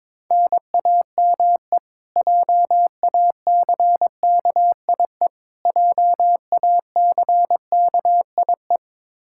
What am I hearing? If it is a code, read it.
Morse code, heard as a keyed tone, about 22 wpm, 700 Hz. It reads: NAME JACKIE JACKIE
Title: Practice QSO - Exchange 1 - Beginning - 22wpm